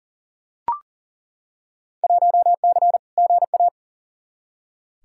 Morse code, heard as a keyed tone, about 40 wpm, 700 Hz. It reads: 1C GA
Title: Field Day Exchanges Encore - 40wpm